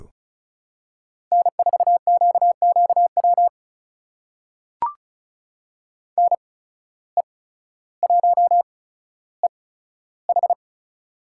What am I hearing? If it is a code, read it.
Morse code, heard as a keyed tone, about 34 wpm, spaced as for 15 wpm, 700 Hz. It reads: N4QQW NE1EH
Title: USA Call Signs - 5 Characters - 8x Character Spacing - 35wpm